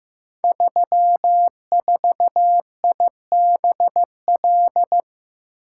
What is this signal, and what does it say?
Morse code, heard as a keyed tone, about 15 wpm, 700 Hz. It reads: 34IBL
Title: Code Groups - 5 Letter:Numbers - 15wpm